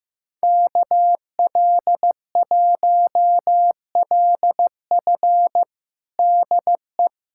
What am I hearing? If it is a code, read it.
Morse code, heard as a keyed tone, about 15 wpm, 700 Hz. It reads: KL1LF DE
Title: USA Callsign DE Callsign - 5 Characters - 15wpm